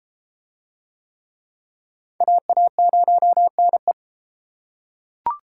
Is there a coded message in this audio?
Yes: Morse code from 2.2 to 3.9 s, heard as a keyed tone, about 32 wpm, 700 Hz. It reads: AA0NE